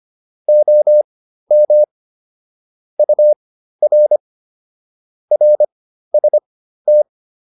Morse code, heard as a keyed tone, about 25 wpm, spaced as for 13 wpm, 600 Hz. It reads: OM UR RST
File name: QSO402 13